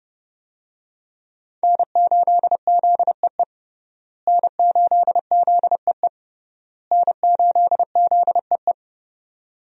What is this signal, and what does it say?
Morse code, heard as a keyed tone, about 32 wpm, 700 Hz. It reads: N8ZEE N8ZEE N8ZEE
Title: USA Call Signs - 5 Characters - Repeated 3x - 30wpm